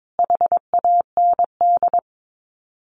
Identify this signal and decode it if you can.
Morse code, heard as a keyed tone, about 22 wpm, 700 Hz. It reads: HAND
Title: Sets of 4 Words 22wpm